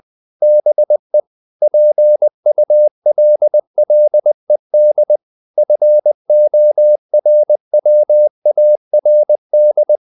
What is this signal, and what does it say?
Morse code, heard as a keyed tone, about 20 wpm, 600 Hz. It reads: BE PULLED FORWARD
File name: the old man and the sea part 2 20wpm